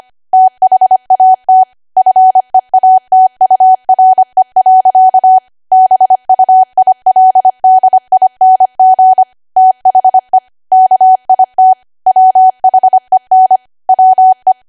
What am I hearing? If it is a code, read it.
Morse code, heard as a keyed tone, about 25 wpm, 750 Hz. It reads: THAT FEATURE. BUILDING THE KIT WHEN WE
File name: cwa / W1AW-Level3 / 25WPM